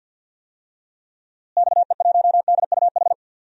Morse code, heard as a keyed tone, about 50 wpm, 700 Hz. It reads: XE1DUH
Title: DXCC-100 Call Signs - 6 Characters - Repeated 2x - 50wpm